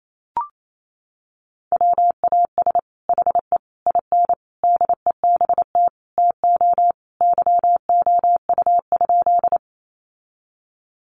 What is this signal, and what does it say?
Morse code, heard as a keyed tone, about 28 wpm, 700 Hz. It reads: WAS HE IN DEBT TO YOU?
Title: Sentences from Top 1300 Words 28wpm